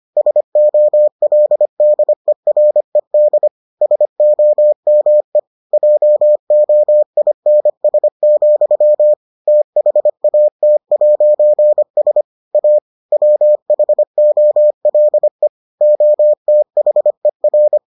Morse code, heard as a keyed tone, about 25 wpm, 600 Hz. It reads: SOLDERED SOME JOINS, THAT'S A WHOLE OTHER